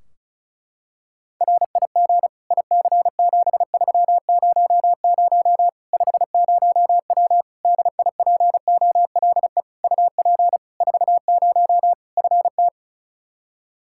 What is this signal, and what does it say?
Morse code, heard as a keyed tone, about 36 wpm, 700 Hz. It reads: RIG IC7300 50W DIPOLE UP 40 FT
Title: Practice QSO - Exchange 2 - Info - 35wpm